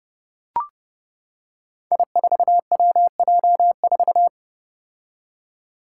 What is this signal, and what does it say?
Morse code, heard as a keyed tone, about 28 wpm, 700 Hz. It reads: I4WJ4
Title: Five Challenging Letter-Number Combinations 30wpm